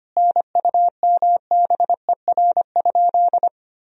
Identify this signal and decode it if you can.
Morse code, heard as a keyed tone, about 25 wpm, 700 Hz. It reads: NUMBER?
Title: Sentences from Top 1000 Words - Review Part 1 - 25wpm